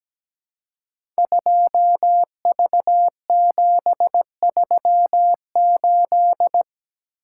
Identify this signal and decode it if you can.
Morse code, heard as a keyed tone, about 17 wpm, 700 Hz. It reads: 2V738
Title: Code Groups - 5 Letter:Numbers - 17wpm